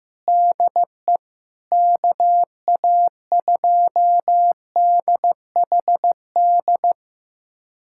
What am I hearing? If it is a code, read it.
Morse code, heard as a keyed tone, about 16 wpm, 700 Hz. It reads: DE KA2DHD
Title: USA Callsign DE Callsign - 6 Characters - 15wpm